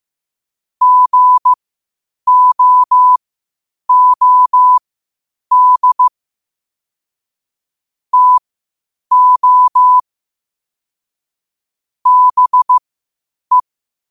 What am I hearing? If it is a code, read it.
Morse code, heard as a keyed tone, about 15 wpm, spaced as for 8 wpm, 1000 Hz. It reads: GOOD TO BE
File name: qotdfiles / 8wpm